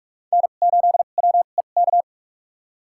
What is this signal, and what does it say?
Morse code, heard as a keyed tone, about 43 wpm, spaced as for 32 wpm, 700 Hz. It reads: N8WEK
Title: USA Call Signs - 2x Character Spacing - 45wpm